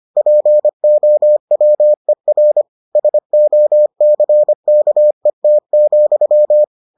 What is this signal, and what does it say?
Morse code, heard as a keyed tone, about 25 wpm, 600 Hz. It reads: POWER SOCKET,